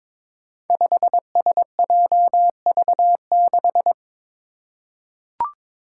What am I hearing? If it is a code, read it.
Morse code, heard as a keyed tone, about 22 wpm, 700 Hz. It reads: 5SJV6
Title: Five Challenging Letter-Number Combinations 22wpm